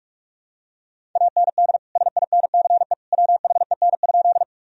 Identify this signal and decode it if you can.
Morse code, heard as a keyed tone, about 47 wpm, 700 Hz. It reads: AND SINCE WHEN?
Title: Sentences from Top 1000 Words - Review Part 1 - 45wpm